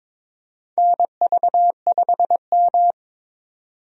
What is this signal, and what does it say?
Morse code, heard as a keyed tone, about 22 wpm, 700 Hz. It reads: NV5M
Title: USA Call Signs - 4 Characters - 22wpm